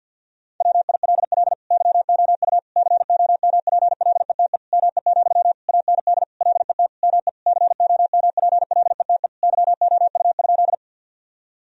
Morse code, heard as a keyed tone, about 50 wpm, 700 Hz. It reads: WILL YOU COMPLETE ME, AND LET ME COMPLETE YOU?
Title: Sentences from Top 1000 Words - Review Part 2 - 50wpm